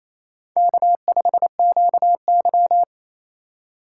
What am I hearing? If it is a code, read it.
Morse code, heard as a keyed tone, about 29 wpm, 700 Hz. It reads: K5QY
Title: USA Call Signs - 4 Characters - 4x Character Spacing - 28wpm